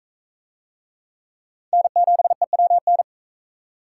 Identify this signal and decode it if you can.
Morse code, heard as a keyed tone, about 40 wpm, 700 Hz. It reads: N7EWN